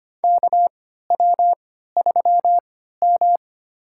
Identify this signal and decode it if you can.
Morse code, heard as a keyed tone, about 25 wpm, spaced as for 14 wpm, 700 Hz. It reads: KW3M
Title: USA Call Signs - 3x Character Spacing - 25wpm